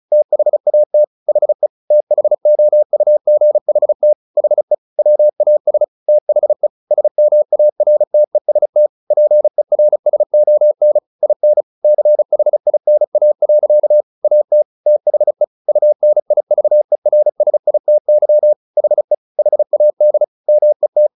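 Morse code, heard as a keyed tone, about 34 wpm, 600 Hz. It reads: THAT HE THOUGHT HE WAS THE SMARTEST PERSON IN CHINA. AT THE UNIVERSITY HE HAD MET
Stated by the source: Morse practice at 35 wpm from CW med Gnister, sending Ken Follett's novel Never in English.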